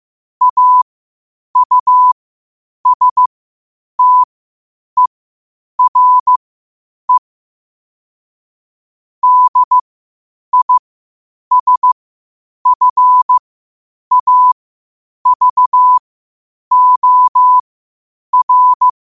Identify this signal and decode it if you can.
Morse code, heard as a keyed tone, about 15 wpm, spaced as for 8 wpm, 1000 Hz. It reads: AUSTERE DISFAVOR
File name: qotdfiles / 8wpm